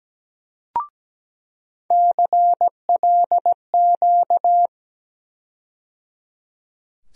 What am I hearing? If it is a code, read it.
Morse code, heard as a keyed tone, about 17 wpm, 700 Hz. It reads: CLQ
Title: USA Call Sign Postfixes - 3 Letters 17wpm